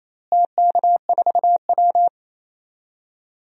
Morse code, heard as a keyed tone, about 28 wpm, 700 Hz. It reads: TK4W
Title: DXCC-100 Call Signs - 4 Characters - Repeated 2x - 28wpm